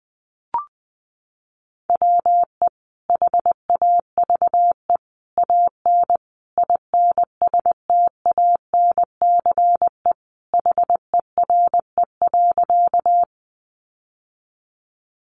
Morse code, heard as a keyed tone, about 20 wpm, 700 Hz. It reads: WE HAVE AN INSTANCE HERE.
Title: Sentences from Top 800 Words 20wpm